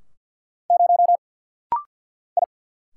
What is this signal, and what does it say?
Morse code, heard as a keyed tone, about 50 wpm, 700 Hz. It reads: 0 I
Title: Single Letter-Number - Rapid-Fire 50wpm